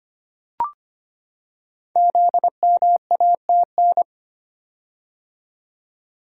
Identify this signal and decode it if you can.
Morse code, heard as a keyed tone, about 25 wpm, 700 Hz. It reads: ZMATN
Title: Code Groups - 5 Letters - 25wpm